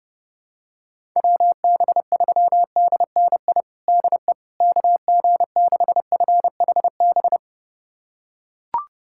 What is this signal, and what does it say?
Morse code, heard as a keyed tone, about 30 wpm, 700 Hz. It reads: WB3DNI DE KG6FHB
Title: USA Callsign DE Callsign - 6 Characters - 30wpm